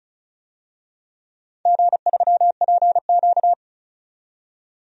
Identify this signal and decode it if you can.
Morse code, heard as a keyed tone, about 36 wpm, 690 Hz. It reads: G3PQ
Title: DXCC-100 Call Signs - 4 Characters - 35wpm